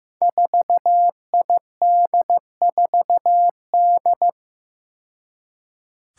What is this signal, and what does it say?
Morse code, heard as a keyed tone, about 15 wpm, 700 Hz. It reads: 4ID4D
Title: Five Challenging Letter-Number Combinations 15wpm